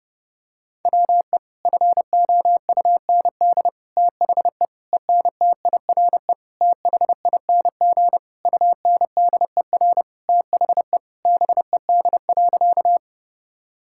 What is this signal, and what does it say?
Morse code, heard as a keyed tone, about 32 wpm, 700 Hz. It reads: WE FOUND THE ENTIRE THING UNDER THE BED.
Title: Sentences from Top 1000 Words - Review Part 5 - 30wpm